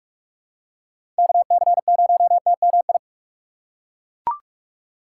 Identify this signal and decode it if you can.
Morse code, heard as a keyed tone, about 42 wpm, 700 Hz. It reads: KC0TMI
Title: USA Call Signs - 6 Characters - 3x Character Spacing - 45wpm